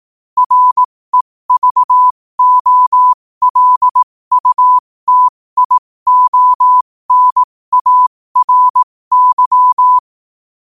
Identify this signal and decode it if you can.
Morse code, heard as a keyed tone, about 18 wpm, spaced as for 15 wpm, 1000 Hz. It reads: REVOLUTIONARY
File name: qotdfiles / 15wpm